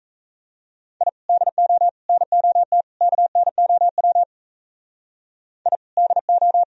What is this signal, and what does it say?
Morse code, heard as a keyed tone, about 34 wpm, 700 Hz. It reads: I DO NOT KNOW I DO
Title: Speed-Racing - Sets of 4 Words 28wpm